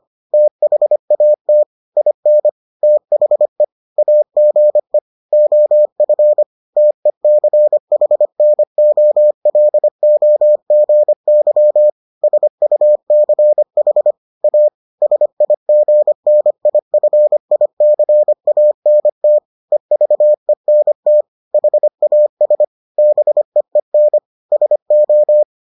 Morse code, heard as a keyed tone, about 25 wpm, 600 Hz. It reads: THAT IN THE AGE OF TECHNOLOGY SUCH A SIGNIFICANT EVENT HAS BEEN SO